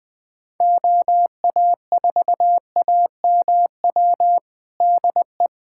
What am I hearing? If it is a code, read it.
Morse code, heard as a keyed tone, about 20 wpm, 700 Hz. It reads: OA4AMW DE
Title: USA Callsign DE Callsign 20wpm